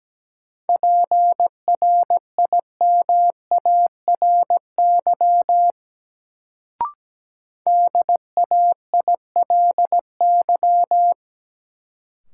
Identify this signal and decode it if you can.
Morse code, heard as a keyed tone, about 17 wpm, 700 Hz. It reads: PRIMARY DAILY
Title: Sentences from Top 1100 Words - New Word Drill Alt - 17wpm